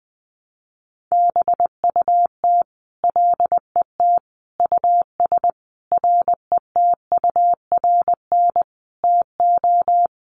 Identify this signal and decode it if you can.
Morse code, heard as a keyed tone, about 20 wpm, 700 Hz. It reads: BUT LET US RETURN TO
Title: Sentences from Top 800 Words 20wpm